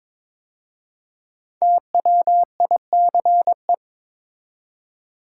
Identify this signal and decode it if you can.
Morse code, heard as a keyed tone, about 22 wpm, 700 Hz. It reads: TWICE
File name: 5 Letter Words - First 3 Letters Given - 22wpm